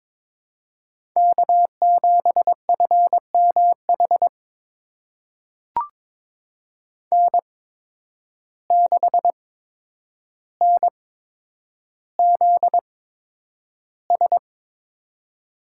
Morse code, heard as a keyed tone, about 22 wpm, spaced as for 9 wpm, 700 Hz. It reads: K7FMH N6NZS
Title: USA Call Signs - 5 Characters - 8x Character Spacing - 22wpm